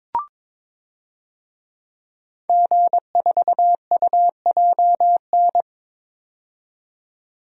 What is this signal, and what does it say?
Morse code, heard as a keyed tone, about 22 wpm, 700 Hz. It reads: G4UJN
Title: DXCC-100 Call Signs - 5 Characters - 22wpm